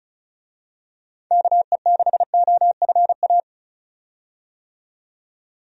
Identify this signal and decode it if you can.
Morse code, heard as a keyed tone, about 36 wpm, 700 Hz. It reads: KE6OFA
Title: USA Call Signs - 6 Characters - 35wpm